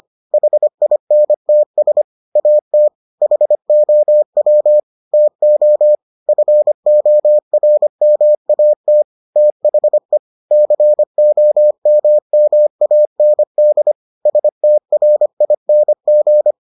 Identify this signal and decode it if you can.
Morse code, heard as a keyed tone, about 25 wpm, 600 Hz. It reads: HINTS AT HOW TO FORMAT THE COMMAND STRING